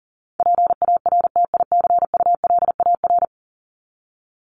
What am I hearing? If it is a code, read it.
Morse code, heard as a keyed tone, about 40 wpm, 700 Hz. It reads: PARTICULAR
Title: LY Suffix - Top 100 Words with Base Words 40wpm